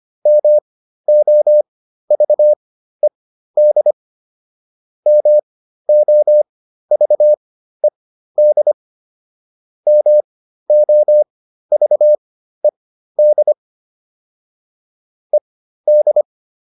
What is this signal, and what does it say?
Morse code, heard as a keyed tone, about 25 wpm, spaced as for 13 wpm, 600 Hz. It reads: MOVED MOVED MOVED ED